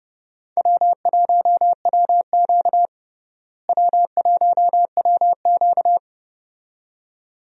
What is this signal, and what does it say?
Morse code, heard as a keyed tone, about 31 wpm, 700 Hz. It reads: W1WQ W1WQ
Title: USA Call Signs - 4 Characters - Repeated 2x - 30wpm